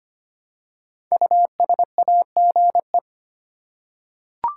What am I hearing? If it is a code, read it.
Morse code, heard as a keyed tone, about 25 wpm, 700 Hz. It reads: USAGE
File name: 5 Letter Words - First 3 Letters Given - 25wpm